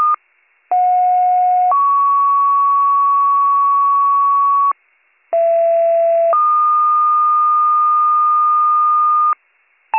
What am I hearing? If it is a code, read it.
Two-tone page, heard as a one-second tone, then a three-second tone, about 730 then 1120 Hz, about 670 then 1220 Hz. It is heard from 0.7 to 9.3 s.